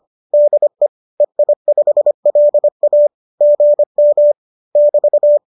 Morse code, heard as a keyed tone, about 25 wpm, 590 Hz. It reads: DE EI5LA GM <BT>